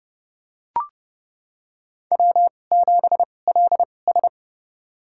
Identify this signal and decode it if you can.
Morse code, heard as a keyed tone, about 29 wpm, spaced as for 21 wpm, 700 Hz. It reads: W7LS
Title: USA Call Signs - 4 Characters - 2x Character Spacing - 30wpm